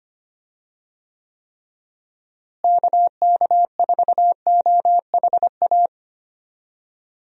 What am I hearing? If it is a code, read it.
Morse code, heard as a keyed tone, about 25 wpm, 700 Hz. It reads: KK4OHA
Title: USA Call Signs - 6 Characters - Repeated 3x - 25wpm